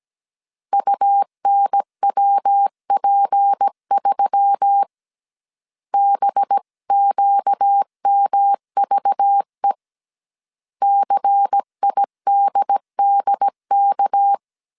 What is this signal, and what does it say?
Morse code, heard as a keyed tone, about 17 wpm, 800 Hz. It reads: UNWP3 BQMVE CIDDK
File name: CW / DET / 1 6 1500C S A 16wpm 2635